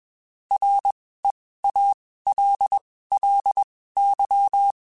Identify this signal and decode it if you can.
Morse code, heard as a keyed tone, about 21 wpm, spaced as for 15 wpm, 800 Hz. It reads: REALLY
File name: (spaced) / QSOs Spaced-mp3s / WD0BGP 1 @20 sp